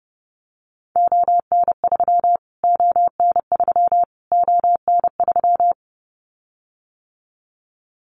Morse code, heard as a keyed tone, about 30 wpm, 700 Hz. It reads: ON3 ON3 ON3
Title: Callsign Prefixes for Top 25 DXCC - Repeated 3x - 30wpm